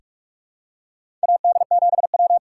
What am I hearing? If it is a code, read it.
Morse code, heard as a keyed tone, about 46 wpm, 700 Hz. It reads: AD7W
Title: USA Call Signs - 4 Characters - 6x Character Spacing - 45wpm